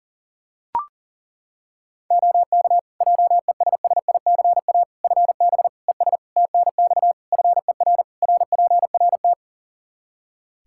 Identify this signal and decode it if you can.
Morse code, heard as a keyed tone, about 40 wpm, 700 Hz. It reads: OK JESSICA FB ES TNX FER RPRT
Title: Practice QSO - Exchange 2 - Info - 40wpm